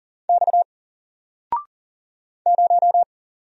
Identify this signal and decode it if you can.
Morse code, heard as a keyed tone, about 40 wpm, 700 Hz. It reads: X 0
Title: Single Letter-Number 40wpm